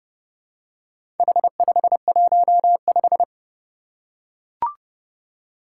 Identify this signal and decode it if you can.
Morse code, heard as a keyed tone, about 30 wpm, 700 Hz. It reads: H515
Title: Four Challenging Letter-Number Combinations 30wpm